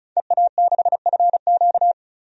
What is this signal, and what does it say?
Morse code, heard as a keyed tone, about 36 wpm, 700 Hz. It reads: EA6FQ
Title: DXCC-100 Call Signs - 5 Characters - 35wpm